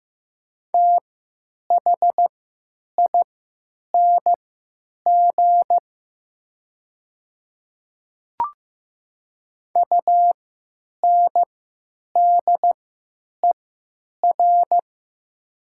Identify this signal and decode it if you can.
Morse code, heard as a keyed tone, about 15 wpm, spaced as for 9 wpm, 700 Hz. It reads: THING UNDER
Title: 3x Character Spacing - Top 300 Words - 15wpm